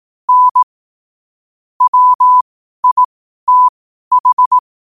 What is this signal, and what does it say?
Morse code, heard as a keyed tone, about 18 wpm, spaced as for 12 wpm, 1000 Hz. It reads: N WITH
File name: qotdfiles / 12wpm